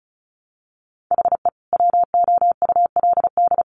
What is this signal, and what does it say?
Morse code, heard as a keyed tone, about 34 wpm, 700 Hz. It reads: HE WOULD
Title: Sets of 2 Words 35wpm